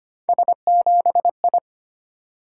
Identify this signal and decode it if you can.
Morse code, heard as a keyed tone, about 25 wpm, 700 Hz. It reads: S7I
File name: Three Letter-Number Combinations - 25wpm